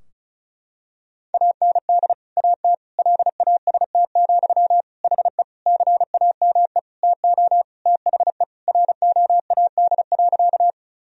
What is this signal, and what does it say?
Morse code, heard as a keyed tone, about 34 wpm, 700 Hz. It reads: AND AT LAST, HE CAME TO THE ROAD.
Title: Sentences from Top 1000 Words - Review Part 2 - 35wpm